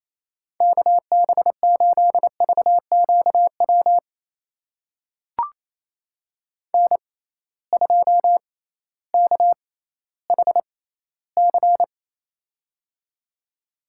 Morse code, heard as a keyed tone, about 28 wpm, spaced as for 15 wpm, 700 Hz. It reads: KB8VQW N2KHC
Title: USA Call Signs - 6x Character Spacing - 28wpm